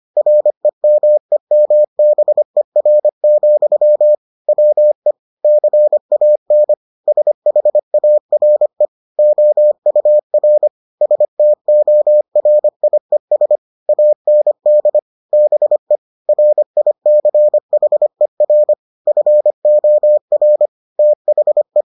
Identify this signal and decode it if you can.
Morse code, heard as a keyed tone, about 25 wpm, 600 Hz. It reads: REMEMBER, WE CAN SHARE OUR STORIES AND BE RICHER FOR THE